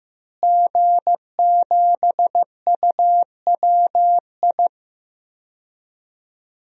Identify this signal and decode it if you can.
Morse code, heard as a keyed tone, about 15 wpm, 700 Hz. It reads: G7UWI
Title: DXCC-100 Call Signs - 5 Characters - Repeated 3x - 15wpm